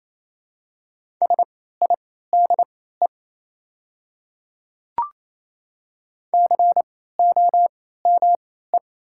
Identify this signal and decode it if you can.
Morse code, heard as a keyed tone, about 27 wpm, spaced as for 16 wpm, 700 Hz. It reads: SIDE COME
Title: 3x Character Spacing - Top 100 Words - 28wpm